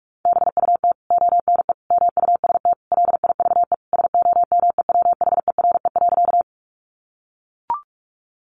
Morse code, heard as a keyed tone, about 44 wpm, 700 Hz. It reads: BUT ONE MUST LIVE SOMEWHERE.
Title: Sentences from Top 1400 Words 45wpm